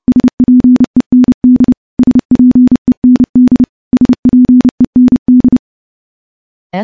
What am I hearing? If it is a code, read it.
Morse code, heard as a keyed tone, about 30 wpm, 250 Hz. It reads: SPEND SPEND SPEND
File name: morse / morse-training-30WPM